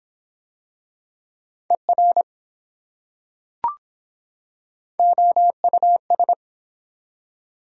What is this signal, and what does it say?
Morse code, heard as a keyed tone, about 24 wpm, 700 Hz. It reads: ER OUS